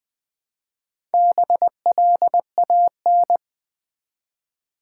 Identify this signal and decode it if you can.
Morse code, heard as a keyed tone, about 20 wpm, 700 Hz. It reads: BLAN
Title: Build A Word - 1 Letter At A Time - 5-Letter Words - 20wpm